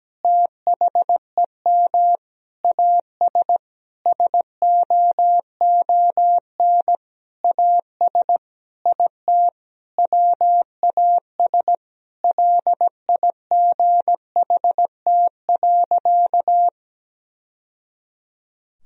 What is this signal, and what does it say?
Morse code, heard as a keyed tone, about 17 wpm, 700 Hz. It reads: THEM AS SOON AS IT WAS LIGHT.
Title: Sentences from Top 1000 Words - Review Part 1 - 17wpm